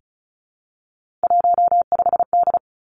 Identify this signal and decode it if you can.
Morse code, heard as a keyed tone, about 36 wpm, 700 Hz. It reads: 15D
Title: Three Challenging Letter-Number Combinations 35wpm